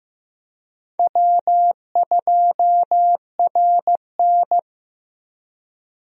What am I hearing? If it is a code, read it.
Morse code, heard as a keyed tone, about 15 wpm, 700 Hz. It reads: W2RN
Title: USA Call Signs - 4 Characters - Repeated 3x - 15wpm